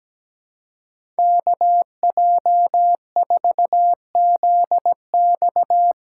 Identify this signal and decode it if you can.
Morse code, heard as a keyed tone, about 17 wpm, 700 Hz. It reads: KJ4ZX
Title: USA Call Signs - 6x Character Spacing - 17wpm